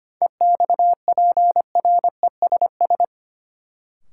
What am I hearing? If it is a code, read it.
Morse code, heard as a keyed tone, about 25 wpm, 700 Hz. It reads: EXPRESS
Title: Sentences from Top 1100 Words - New Word Drill - 25wpm